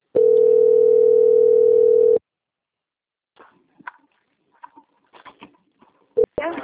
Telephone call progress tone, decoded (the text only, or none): ringback tone